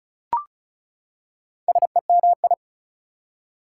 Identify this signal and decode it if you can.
Morse code, heard as a keyed tone, about 36 wpm, 700 Hz. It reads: SEMI
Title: Prefixes and Suffixes 35wpm